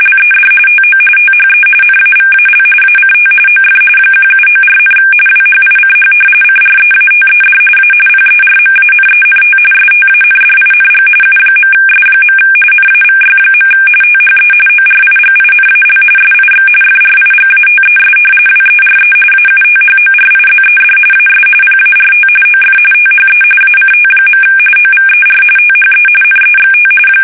RTTY, about 45.45 baud. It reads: LED TO THE NAVAL ACT OF 1794, WHICH CREATED A PERMANENT STANDING U.S. NAVY. THE ORIGINAL SIX FRIGATES WERE AUTHORIZED AS PART OF THE ACT. OVER THE NEXT
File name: fsk850 / usnavy1